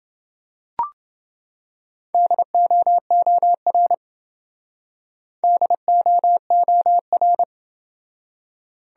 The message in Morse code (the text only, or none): DOOR DOOR